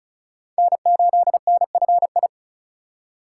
Morse code, heard as a keyed tone, about 36 wpm, 690 Hz. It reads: N8NFI
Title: USA Call Signs - 5 Characters - 3x Character Spacing - 35wpm